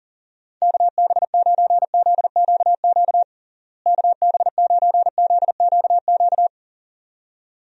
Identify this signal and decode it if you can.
Morse code, heard as a keyed tone, about 40 wpm, 700 Hz. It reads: KB9ZQQ KB9ZQQ
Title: USA Call Signs - 6 Characters - Repeated 2x - 40wpm